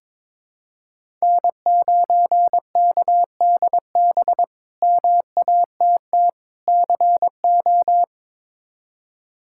Morse code, heard as a keyed tone, about 22 wpm, 700 Hz. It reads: N9KDB MATT CO
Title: K1USN SST QSO Practice - 22wpm